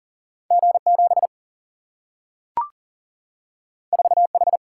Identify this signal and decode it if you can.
Morse code, heard as a keyed tone, about 40 wpm, 700 Hz. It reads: G 7 4 H
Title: Single-Letter Single-Number 40wpm